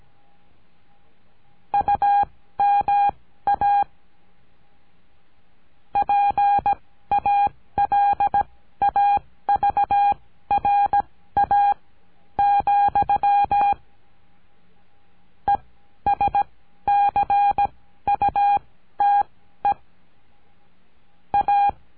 Morse code, heard as a keyed tone, about 17 wpm, spaced as for 12 wpm, 810 Hz. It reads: UMA PALAVRA, ESCUTE A
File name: CURSO DE TELEGRAFIA